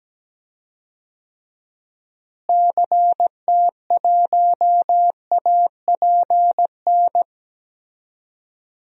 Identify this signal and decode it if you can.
Morse code, heard as a keyed tone, about 17 wpm, 700 Hz. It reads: CT1APN
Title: DXCC-100 Call Signs - 6 Characters - Repeated 2x - 17wpm